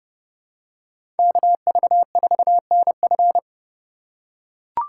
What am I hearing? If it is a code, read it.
Morse code, heard as a keyed tone, about 32 wpm, 700 Hz. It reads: KV4NF
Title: USA Call Signs - 3x Character Spacing - 30wpm